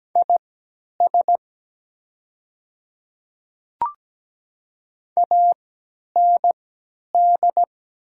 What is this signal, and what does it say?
Morse code, heard as a keyed tone, about 17 wpm, spaced as for 10 wpm, 700 Hz. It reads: IS AND
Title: 3x Character Spacing - Top 100 Words - 17wpm